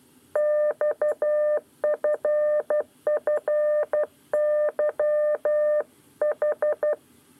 Morse code, heard as a keyed tone, about 12 wpm, 600 Hz. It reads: XFFYH
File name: 9 lezione cw 2